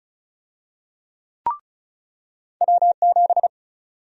Morse code, heard as a keyed tone, about 36 wpm, 700 Hz. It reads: W7
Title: USA Call Sign Prefixes - 35wpm